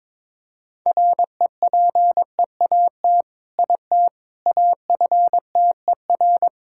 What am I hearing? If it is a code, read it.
Morse code, heard as a keyed tone, about 22 wpm, 700 Hz. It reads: REPEAT IT AFTER
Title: Sentences from Top 1600 Words 22wpm